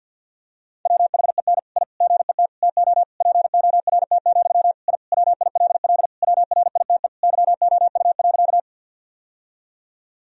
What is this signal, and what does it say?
Morse code, heard as a keyed tone, about 52 wpm, 700 Hz. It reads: WHEN I GET TO PORT, I WILL WRITE YOU.
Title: Sentences from Top 1000 Words - Review Part 1 - 50wpm